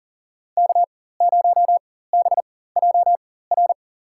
Morse code, heard as a keyed tone, about 40 wpm, spaced as for 19 wpm, 700 Hz. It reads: K0BJR
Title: USA Call Signs - 5 Characters - 4x Character Spacing - 40wpm